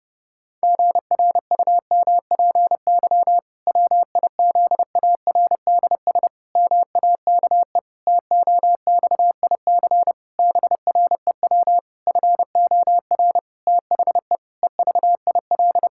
Morse code, heard as a keyed tone, about 28 wpm, 700 Hz. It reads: GRUMPY WIZARDS MAKE TOXIC BREW FOR THE EVIL
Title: Pangram Challenge v2 - 30wpm